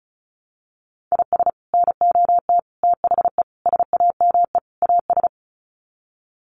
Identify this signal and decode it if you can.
Morse code, heard as a keyed tone, about 34 wpm, 700 Hz. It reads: IS NOT THE SAME AS
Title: Sets of 5 Words Encore 35wpm